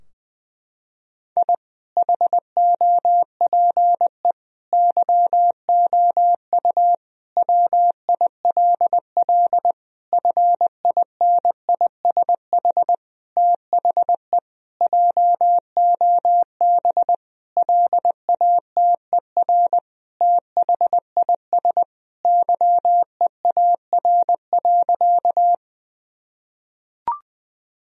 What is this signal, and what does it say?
Morse code, heard as a keyed tone, about 20 wpm, 700 Hz. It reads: I HOPE YOU WILL FINISH THE JOB LATER THIS YEAR.
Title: Sentences from Top 1000 Words - Review Part 3 - 20wpm